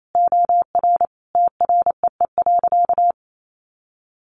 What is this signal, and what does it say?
Morse code, heard as a keyed tone, about 28 wpm, 700 Hz. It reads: OR TREE.
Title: A Sound Of Thunder 28wpm